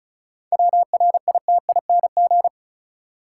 Morse code, heard as a keyed tone, about 34 wpm, 700 Hz. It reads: WRITING
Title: Seven Letter Words - 35wpm